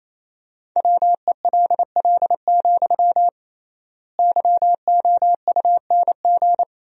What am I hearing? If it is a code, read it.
Morse code, heard as a keyed tone, about 28 wpm, 700 Hz. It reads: WELL, YOUNG
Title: 3x Word-Spacing - Sentences from Top 400 Words - 28wpm